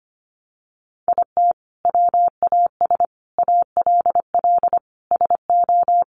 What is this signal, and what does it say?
Morse code, heard as a keyed tone, about 25 wpm, 700 Hz. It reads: IT WAS ALL SO
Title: Sentences from Top 1000 Words 25wpm